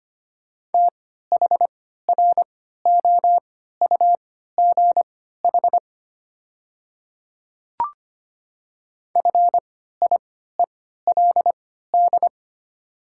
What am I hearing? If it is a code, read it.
Morse code, heard as a keyed tone, about 25 wpm, spaced as for 14 wpm, 700 Hz. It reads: THROUGH FIELD
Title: 3x Character Spacing - Top 500 Words - 25wpm